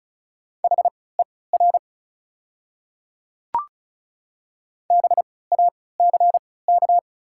Morse code, heard as a keyed tone, about 36 wpm, spaced as for 20 wpm, 700 Hz. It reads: HER BACK
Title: 3x Character Spacing - Top 200 Words - 35wpm